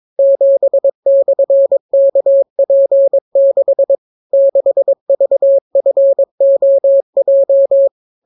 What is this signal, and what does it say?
Morse code, heard as a keyed tone, about 22 wpm, 550 Hz. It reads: 7/KP6 6VFOJ